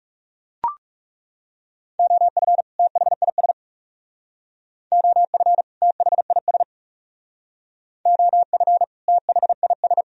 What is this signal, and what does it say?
Morse code, heard as a keyed tone, about 36 wpm, 700 Hz. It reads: OF THIS OF THIS OF THIS
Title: Speed-Racing - Sets of 2 Words 30wpm